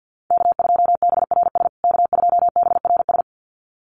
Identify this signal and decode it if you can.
Morse code, heard as a keyed tone, about 50 wpm, 700 Hz. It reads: K2BRS K2BRS
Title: USA Call Signs Repeated 2x 50wpm